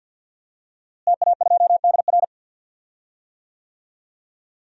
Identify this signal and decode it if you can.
Morse code, heard as a keyed tone, about 52 wpm, 700 Hz. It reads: TA2DR
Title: DXCC-100 Call Signs - 5 Characters - Repeated 2x - 50wpm